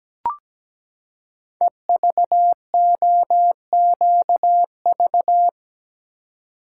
Morse code, heard as a keyed tone, about 17 wpm, 700 Hz. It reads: EVOQV
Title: Code Groups - 5 Letters - 17wpm